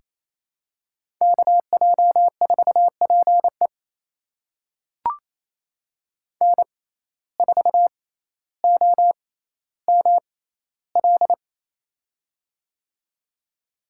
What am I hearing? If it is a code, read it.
Morse code, heard as a keyed tone, about 28 wpm, spaced as for 14 wpm, 700 Hz. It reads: KJ4PE N4OML
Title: USA Call Signs - 5 Characters - 6x Character Spacing - 28wpm